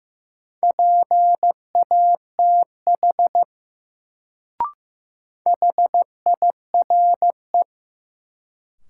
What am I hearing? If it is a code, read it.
Morse code, heard as a keyed tone, about 15 wpm, 700 Hz. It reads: PATH HIRE
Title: Sentences from Top 1400 Words - New Word Drill Alt - 15wpm